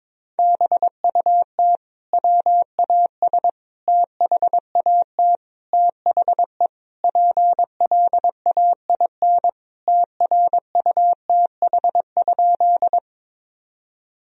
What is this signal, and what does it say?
Morse code, heard as a keyed tone, about 22 wpm, 700 Hz. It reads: BUT WAS THAT THE PLAIN TRUTH?
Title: Sentences from Top 1000 Words - Review Part 2 - 22wpm